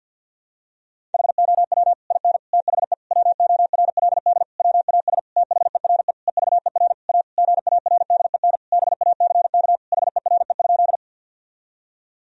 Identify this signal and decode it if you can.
Morse code, heard as a keyed tone, about 51 wpm, 700 Hz. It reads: HOW IN THE WORLD WAS THERE EVER A GARDEN BACK HERE?
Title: Sentences from Top 1000 Words - Review Part 4 - 50wpm